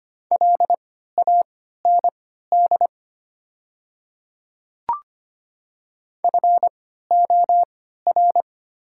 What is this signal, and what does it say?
Morse code, heard as a keyed tone, about 25 wpm, spaced as for 14 wpm, 690 Hz. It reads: LAND FOR
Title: 3x Character Spacing - Top 500 Words - 25wpm